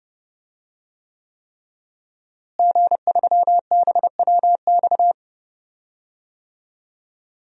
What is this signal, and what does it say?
Morse code, heard as a keyed tone, about 30 wpm, 700 Hz. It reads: G3BWX
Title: DXCC-100 Call Signs - 5 Characters - Repeated 3x - 30wpm